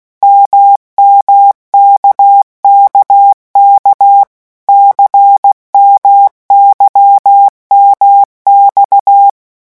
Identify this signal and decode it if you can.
Morse code, heard as a keyed tone, about 16 wpm, 800 Hz. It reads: MMKKK CMYMX
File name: morse mp3 / revision1-vitesse 15 mots